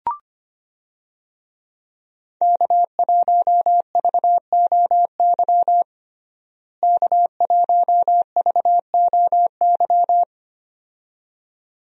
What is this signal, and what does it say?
Morse code, heard as a keyed tone, about 25 wpm, 700 Hz. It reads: K1VOY K1VOY
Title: USA Call Signs - 5 Characters - Repeated 2x - 25wpm